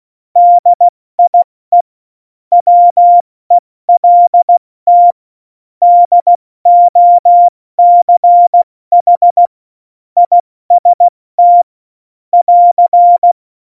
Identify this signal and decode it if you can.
Morse code, heard as a keyed tone, about 16 wpm, 700 Hz. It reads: DIE WELT DOCH IST <AR>
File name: Morsekurs / Witze / Witz 10.Juli 2018, 70 F80